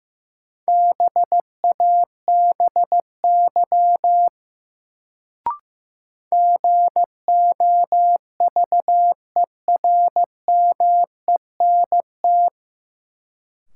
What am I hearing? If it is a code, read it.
Morse code, heard as a keyed tone, about 15 wpm, 700 Hz. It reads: BABY GOVERMENT
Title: Sentences from Top 1300 Words - New Word Drill - 15wpm